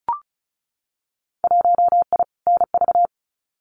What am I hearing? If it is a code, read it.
Morse code, heard as a keyed tone, about 36 wpm, 700 Hz. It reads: 1I NV
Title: Winter Field Day Exchanges Encore 35wpm